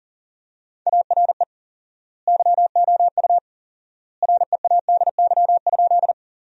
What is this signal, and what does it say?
Morse code, heard as a keyed tone, about 40 wpm, 700 Hz. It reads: ARE YOU READY?
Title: 4x Word-Spacing - Sentences from Top 400 Words - 40wpm